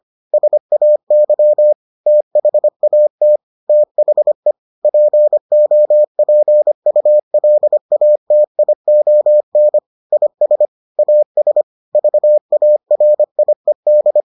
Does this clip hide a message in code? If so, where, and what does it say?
Yes: Morse code from 0.3 to 14.2 s, heard as a keyed tone, about 25 wpm, 600 Hz. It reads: SAY THAT THE POPULATION IS AS VARIED